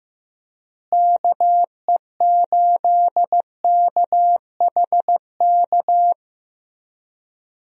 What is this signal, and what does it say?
Morse code, heard as a keyed tone, about 15 wpm, 700 Hz. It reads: KE8KHK
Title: USA Call Signs - 6 Characters - Repeated 3x - 15wpm